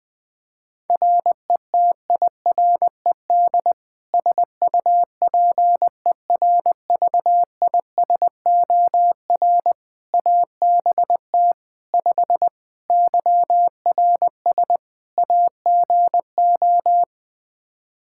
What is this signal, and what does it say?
Morse code, heard as a keyed tone, about 20 wpm, 700 Hz. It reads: RETIRED SUPERVISOR ABT 5 YRS AGO
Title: Practice QSO - Exchange 3 - More Info - 20wpm